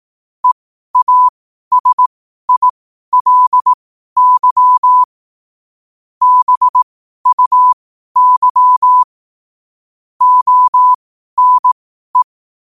Morse code, heard as a keyed tone, about 18 wpm, spaced as for 12 wpm, 1000 Hz. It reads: EASILY BUY ONE
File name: qotdfiles / 12wpm